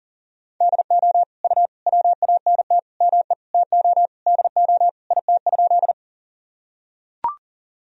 Morse code, heard as a keyed tone, about 40 wpm, 700 Hz. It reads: DO U WANT ME TO DO IT?
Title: Sentences from Top 200 Words and Abbreviations 40wpm